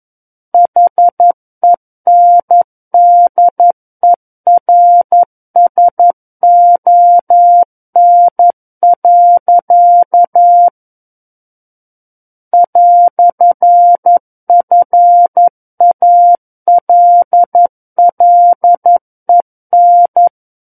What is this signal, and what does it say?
Morse code, heard as a keyed tone, about 11 wpm, 700 Hz. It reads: HENDERSON. "FALLEN
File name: 11 WPM